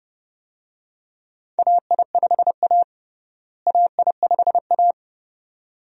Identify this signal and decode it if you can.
Morse code, heard as a keyed tone, about 30 wpm, 700 Hz. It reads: AI5A AI5A
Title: USA Call Signs - 4 Characters - Repeated 2x - 30wpm